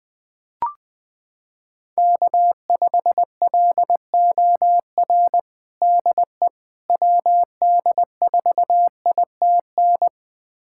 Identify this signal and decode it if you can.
Morse code, heard as a keyed tone, about 20 wpm, 700 Hz. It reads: K5LOR DE WD4ITN
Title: USA Callsign DE Callsign 20wpm